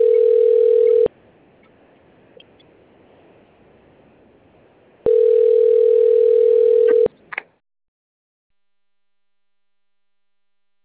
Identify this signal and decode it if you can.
Ringback tone (North America), 440 + 480 Hz.
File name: rg-1600-unknown- - - .59662